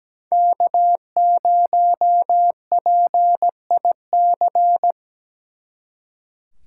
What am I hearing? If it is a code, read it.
Morse code, heard as a keyed tone, about 20 wpm, 700 Hz. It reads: K0PIC
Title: Speed Racing - USA Call Signs 17wpm